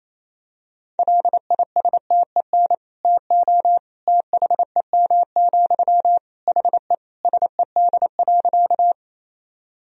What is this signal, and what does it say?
Morse code, heard as a keyed tone, about 28 wpm, 700 Hz. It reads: LISTEN TO THEM, HE SED.
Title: Sentences from Top 400 Words and Abbreviations 28wpm